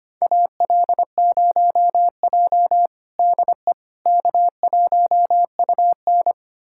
Morse code, heard as a keyed tone, about 25 wpm, 700 Hz. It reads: AL0J DE K1UN
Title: USA Callsign DE Callsign - 4 Characters - 25wpm